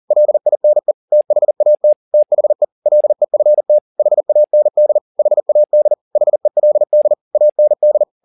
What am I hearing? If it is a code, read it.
Morse code, heard as a keyed tone, about 40 wpm, 600 Hz. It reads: LINE THAT THE LEFT HAND HAD HELD AND